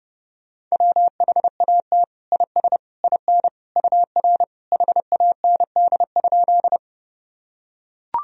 Morse code, heard as a keyed tone, about 32 wpm, 700 Hz. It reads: WHAT IS IN UR HAND?
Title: Sentences from Top 200 Words and Abbreviations 30wpm